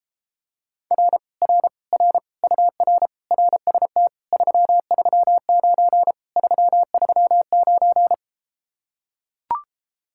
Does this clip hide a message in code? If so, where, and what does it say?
Yes: Morse code from 0.9 to 8.1 s, heard as a keyed tone, about 33 wpm, 700 Hz. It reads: R R R UR RST 339 339